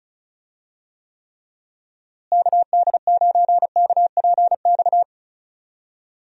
Morse code, heard as a keyed tone, about 34 wpm, 700 Hz. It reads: KD9KPX
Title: USA Call Signs - 6 Characters - Repeated 3x - 35wpm